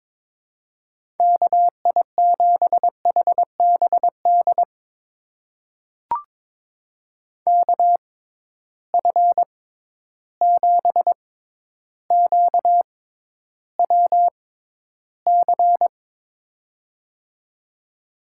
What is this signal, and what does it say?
Morse code, heard as a keyed tone, about 22 wpm, spaced as for 11 wpm, 700 Hz. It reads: KI7HBD KF7QWC
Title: USA Call Signs - 6 Characters - 6x Character Spacing - 22wpm